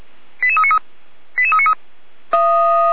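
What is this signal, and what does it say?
Five-tone selective call (ZVEI), heard as quick runs of five tones, tones of about 70 ms. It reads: 88393 88393
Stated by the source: ZVEI five-tone selective call received by SDR, fire brigade alerting, Feuerwehr Orlamünde.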